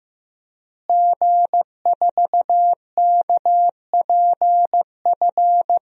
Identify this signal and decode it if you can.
Morse code, heard as a keyed tone, about 15 wpm, 700 Hz. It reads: G4KPF
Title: DXCC-100 Call Signs - 5 Characters - Repeated 3x - 15wpm